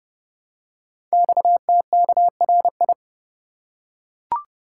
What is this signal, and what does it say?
Morse code, heard as a keyed tone, about 28 wpm, 700 Hz. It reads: XTKRI
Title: Code Groups - 5 Letters - 30wpm